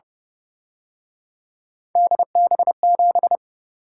Morse code, heard as a keyed tone, about 29 wpm, 700 Hz. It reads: DB7
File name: Speed-Racing - Three Letter-Number Combinations - 25wpm